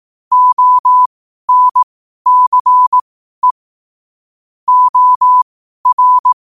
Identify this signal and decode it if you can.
Morse code, heard as a keyed tone, about 18 wpm, spaced as for 12 wpm, 1000 Hz. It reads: ONCE OR
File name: qotdfiles / 12wpm